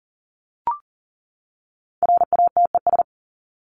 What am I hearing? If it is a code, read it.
Morse code, heard as a keyed tone, about 40 wpm, 700 Hz. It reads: RATES
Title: ES:S Suffix - Top 100 Words with Base Words 40wpm